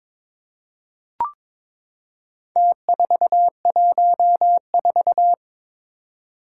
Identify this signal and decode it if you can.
Morse code, heard as a keyed tone, about 21 wpm, 700 Hz. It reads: T414
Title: Four Challenging Letter-Number Combinations 22wpm